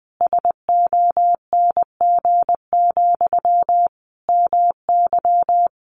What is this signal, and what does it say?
Morse code, heard as a keyed tone, about 20 wpm, 700 Hz. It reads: SONG, MY
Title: Sentences from Top 1500 Words 20wpm